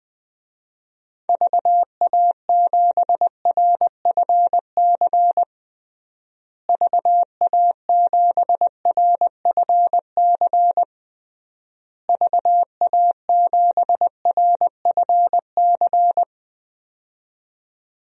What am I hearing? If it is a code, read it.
Morse code, heard as a keyed tone, about 20 wpm, 700 Hz. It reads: VA7RFC VA7RFC VA7RFC
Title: DXCC-100 Call Signs - 6 Characters - Repeated 3x - 20wpm